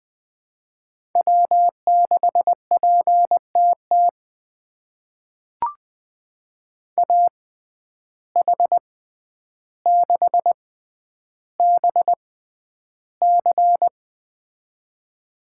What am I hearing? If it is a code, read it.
Morse code, heard as a keyed tone, about 21 wpm, spaced as for 10 wpm, 700 Hz. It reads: W6PTT AH6BC
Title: USA Call Signs - 5 Characters - 6x Character Spacing - 20wpm